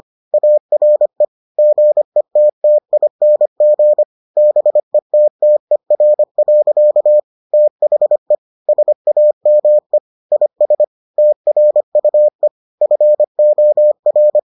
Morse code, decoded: ARE GETTING BETTER. THE SAME IS TRUE FOR

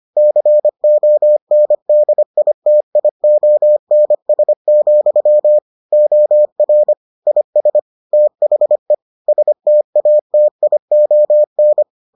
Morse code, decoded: CONDITIONS, OR IS THE STATION

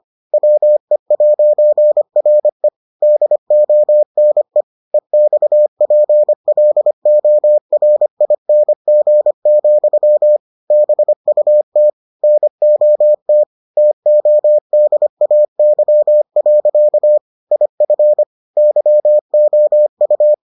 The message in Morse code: WE'RE DONE EXPLORING, BUT NOT TODAY. IF YOU

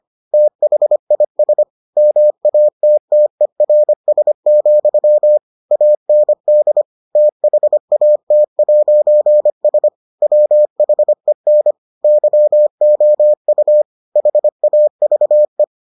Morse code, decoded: THIS MATTERS, AND THAT'S WHEN YOU HAVE

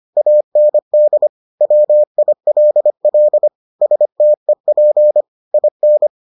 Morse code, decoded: AND WILL STEP IN

